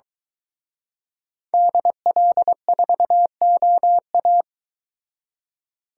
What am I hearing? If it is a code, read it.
Morse code, heard as a keyed tone, about 23 wpm, 700 Hz. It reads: DL4OA